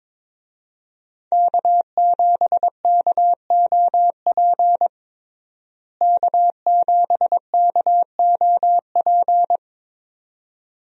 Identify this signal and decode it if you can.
Morse code, heard as a keyed tone, about 22 wpm, 700 Hz. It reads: K7KOP K7KOP
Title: USA Call Signs - 5 Characters - Repeated 2x - 22wpm